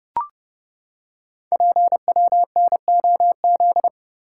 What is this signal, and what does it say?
Morse code, heard as a keyed tone, about 28 wpm, 690 Hz. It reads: PWNOZ
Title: Code Groups - 5 Letters - 30wpm